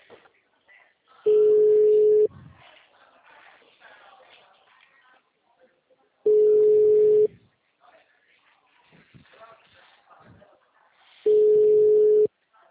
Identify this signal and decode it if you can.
Ringback tone (Europe, 425 Hz), with speech.